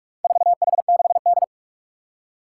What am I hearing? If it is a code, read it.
Morse code, heard as a keyed tone, about 44 wpm, 700 Hz. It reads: 4H6D
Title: Four Challenging Letter-Number Combinations 45wpm